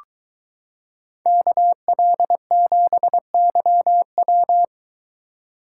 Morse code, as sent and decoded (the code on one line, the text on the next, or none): -.- .-.. --... -.-- .--
KL7YW